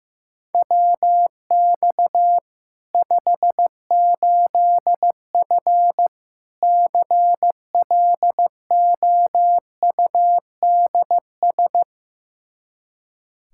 Morse code, heard as a keyed tone, about 15 wpm, 700 Hz. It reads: WX 58F CLOUDS
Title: Practice QSO - Exchange 2 - Info - 15wpm